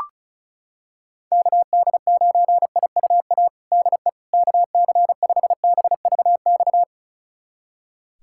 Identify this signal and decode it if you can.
Morse code, heard as a keyed tone, about 36 wpm, 700 Hz. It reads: KD9IUA DE KC5BVX
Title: USA Callsign DE Callsign 35wpm